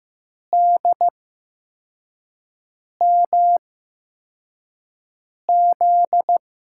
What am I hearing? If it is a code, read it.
Morse code, heard as a keyed tone, about 15 wpm, spaced as for 4 wpm, 700 Hz. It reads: DMZ
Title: USA Call Signs - 6 Characters - 8x Character Spacing - 15wpm